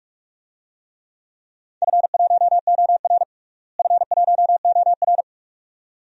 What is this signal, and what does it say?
Morse code, heard as a keyed tone, about 43 wpm, 700 Hz. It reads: F1OR F1OR
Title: DXCC-100 Call Signs - 4 Characters - Repeated 2x - 45wpm